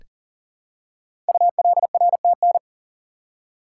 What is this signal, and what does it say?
Morse code, heard as a keyed tone, about 39 wpm, 700 Hz. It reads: ULRTN
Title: Code Groups - 5 Letters - 40wpm